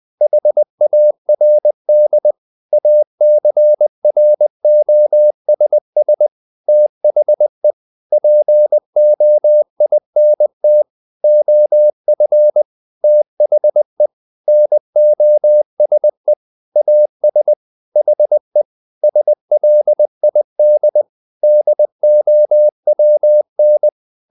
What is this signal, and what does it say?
Morse code, heard as a keyed tone, about 20 wpm, 600 Hz. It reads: HARD ACROSS THE POINT OF THE NOSE AS HE SLID DOWN
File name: the old man and the sea part 2 20wpm